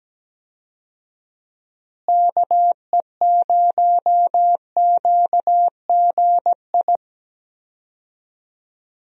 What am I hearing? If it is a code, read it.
Morse code, heard as a keyed tone, about 17 wpm, 700 Hz. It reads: KE0QGI
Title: USA Call Signs - 6 Characters - Repeated 2x - 17wpm